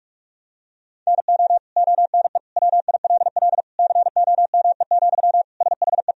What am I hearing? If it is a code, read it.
Morse code, heard as a keyed tone, about 46 wpm, 700 Hz. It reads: NO ONE WILL COME, SHE
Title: Sentences from Top 100 Words and Abbreviations 45wpm